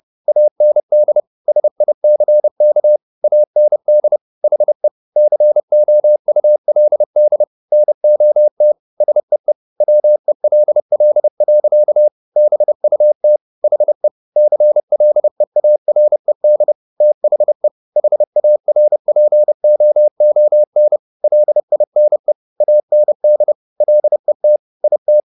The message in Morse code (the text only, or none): AND SICK AND HE COULD NOT SEE WELL. BUT HE CLEARED THE HARPOON LINE AND LET IT